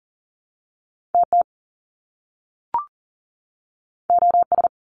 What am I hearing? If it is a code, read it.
Morse code, heard as a keyed tone, about 40 wpm, 700 Hz. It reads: TT OS
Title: Top Two Letter Combinations in Words 40wpm